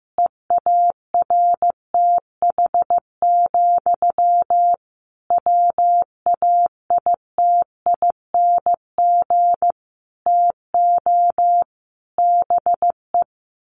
Morse code, heard as a keyed tone, about 14 wpm, 700 Hz. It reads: EARTH, WAITING TO BE
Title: A Sound Of Thunder 15wpm